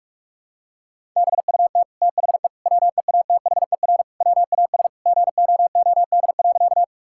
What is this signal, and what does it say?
Morse code, heard as a keyed tone, about 45 wpm, 700 Hz. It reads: BUT THE WEATHER WAS GOOD.
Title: Sentences from Top 1000 Words - Review Part 5 - 45wpm